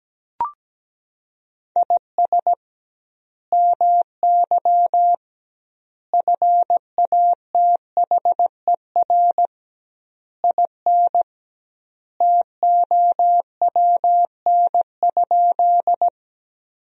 Morse code, heard as a keyed tone, about 17 wpm, 700 Hz. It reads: IS MY FATHER IN TOWN?
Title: 2x Word-Spacing - Sentences from Top 500 Words - 17wpm